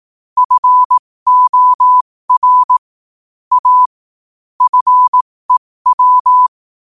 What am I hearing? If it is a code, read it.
Morse code, heard as a keyed tone, about 18 wpm, spaced as for 15 wpm, 1000 Hz. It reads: FOR A FEW
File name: qotdfiles / 15wpm